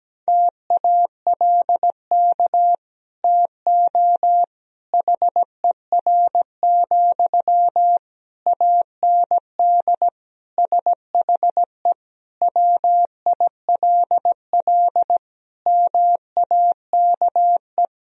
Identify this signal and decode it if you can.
Morse code, heard as a keyed tone, about 17 wpm, 700 Hz. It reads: TALK TO HER, AND SHE WILL MAKE
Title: Sentences from Top 1000 Words - Review Part 3 - 17wpm